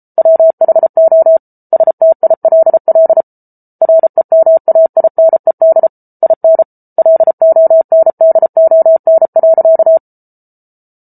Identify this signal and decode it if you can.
Morse code, heard as a keyed tone, about 34 wpm, 650 Hz. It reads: WHO STILL REMAINED IN LONDON.